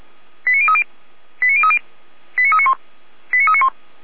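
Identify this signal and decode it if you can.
Five-tone selective call (ZVEI), heard as quick runs of five tones, tones of about 70 ms. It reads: 89030 89030 89391 89391
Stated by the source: ZVEI five-tone call 89504 received by SDR on channel 455.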